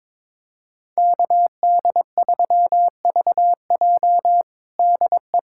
Morse code, heard as a keyed tone, about 22 wpm, 700 Hz. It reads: KD3VJ DE
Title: USA Callsign DE Callsign - 5 Characters - 22wpm